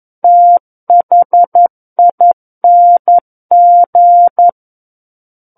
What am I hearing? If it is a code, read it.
Morse code, heard as a keyed tone, about 11 wpm, 700 Hz. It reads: THING
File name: 11 WPM